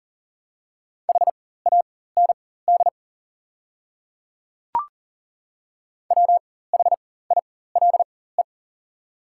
Morse code, heard as a keyed tone, about 40 wpm, spaced as for 19 wpm, 700 Hz. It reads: HAND WHILE
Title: 4x Character Spacing - Top 300 Words - 40wpm